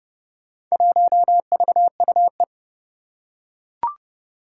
Morse code, heard as a keyed tone, about 30 wpm, 700 Hz. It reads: 1VUE
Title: Four Challenging Letter-Number Combinations 30wpm